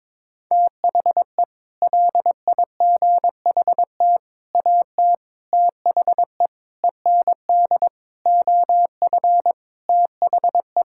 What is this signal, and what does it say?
Morse code, heard as a keyed tone, about 22 wpm, 700 Hz. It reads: THE LIGHT AT THE END OF THE
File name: practice / 22wpm / Sets of 8 Words - 22wpm